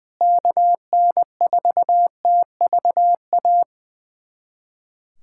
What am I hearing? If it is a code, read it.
Morse code, heard as a keyed tone, about 20 wpm, 700 Hz. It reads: KN4TVA
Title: USA Callsign DE Callsign 20wpm